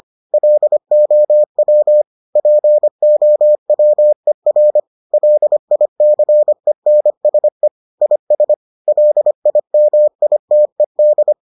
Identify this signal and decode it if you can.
Morse code, heard as a keyed tone, about 25 wpm, 600 Hz. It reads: LOW POWER LICENSE IS LIMITED